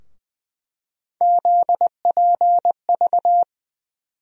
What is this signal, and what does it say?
Morse code, heard as a keyed tone, about 20 wpm, 700 Hz. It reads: ZPV
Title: USA Call Sign Postfixes - 3 Letters 20wpm